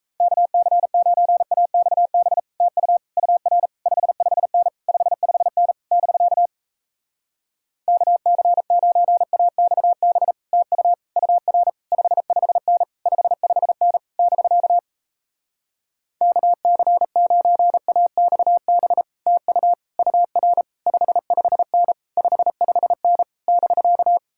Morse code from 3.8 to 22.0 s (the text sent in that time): 55N 55N <BK> KC9AXB TU UR 55N 55N <BK> KC9AXB TU UR 55N